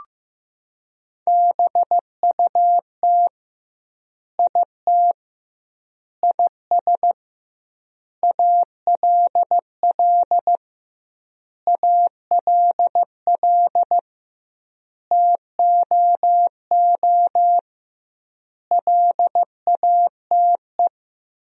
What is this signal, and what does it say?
Morse code, heard as a keyed tone, about 15 wpm, 700 Hz. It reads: BUT IT IS ALL ALL TOO LATE
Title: 2x Word-Spacing - Sentences from Top 300 Words - 15wpm